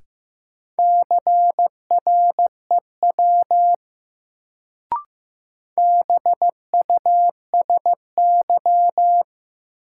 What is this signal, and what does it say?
Morse code, heard as a keyed tone, about 15 wpm, 700 Hz. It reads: CREW BUSY
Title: Sentences from Top 1500 Words - New Word Drill Alt - 15wpm